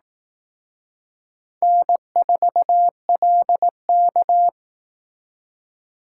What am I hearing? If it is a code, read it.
Morse code, heard as a keyed tone, about 18 wpm, 700 Hz. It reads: N4LK
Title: Speed Racing - USA 1x2 Call Signs 15wpm